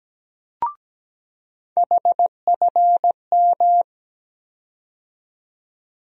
Morse code, heard as a keyed tone, about 17 wpm, 700 Hz. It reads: HFM
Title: USA Call Sign Postfixes - 3 Letters 17wpm